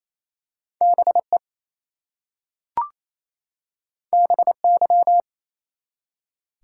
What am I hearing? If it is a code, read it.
Morse code, heard as a keyed tone, about 28 wpm, 700 Hz. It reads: BE BY
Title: Two Letter Words - 28wpm